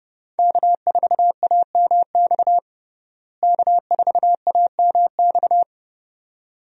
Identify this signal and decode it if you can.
Morse code, heard as a keyed tone, about 30 wpm, 700 Hz. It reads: K4AMX K4AMX
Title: USA Call Signs - 5 Characters - Repeated 2x - 30wpm